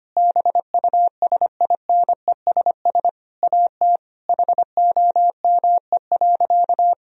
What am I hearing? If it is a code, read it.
Morse code, heard as a keyed tone, about 25 wpm, 700 Hz. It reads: BUSINESS AT HOME.
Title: Sentences from Top 1000 Words - Review Part 4 - 25wpm